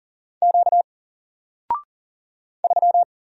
Morse code, heard as a keyed tone, about 40 wpm, 700 Hz. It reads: Q 3
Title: Single Letter-Number Alt 40wpm